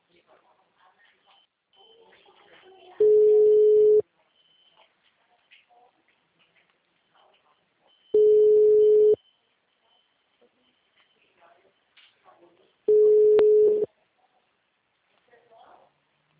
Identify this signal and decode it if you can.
Ringback tone (Europe, 425 Hz), with speech.
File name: out- -2014- - - .22009